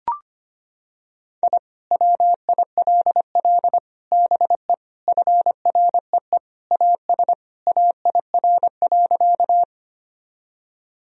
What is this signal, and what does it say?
Morse code, heard as a keyed tone, about 25 wpm, 700 Hz. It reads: I WILL BE FREE AS AIR.
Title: Sentences from Top 1000 Words - Review Part 1 - 25wpm